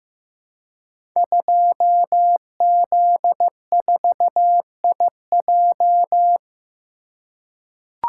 Morse code, heard as a keyed tone, about 15 wpm, 700 Hz. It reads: 2Z4IJ
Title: Code Groups - 5 Letter:Numbers - 15wpm